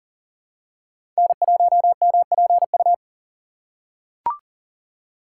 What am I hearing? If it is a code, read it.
Morse code, heard as a keyed tone, about 40 wpm, 700 Hz. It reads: N1MPU
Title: USA Call Signs - 5 Characters - 6x Character Spacing - 40wpm